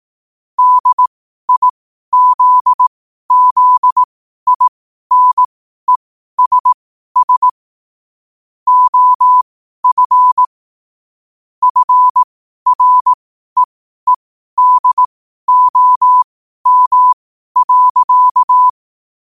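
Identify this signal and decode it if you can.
Morse code, heard as a keyed tone, about 18 wpm, spaced as for 12 wpm, 1000 Hz. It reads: DIZZINESS OF FREEDOM.
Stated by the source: Morse code quote of the day at 12 wpm.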